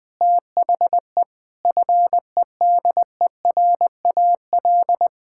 Morse code, heard as a keyed tone, about 20 wpm, 700 Hz. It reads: THE FEDERAL
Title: Sentences from Top 1000 Words - Review Part 6 - 20wpm